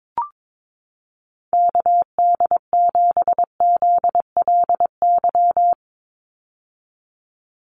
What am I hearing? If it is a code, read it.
Morse code, heard as a keyed tone, about 22 wpm, 700 Hz. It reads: KD7ZLY
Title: USA Call Signs 22wpm